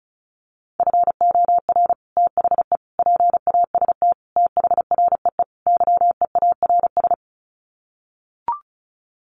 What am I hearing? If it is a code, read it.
Morse code, heard as a keyed tone, about 34 wpm, 700 Hz. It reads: FOR THE PAST THREE YEARS
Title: Sets of 5 Words Encore2 35wpm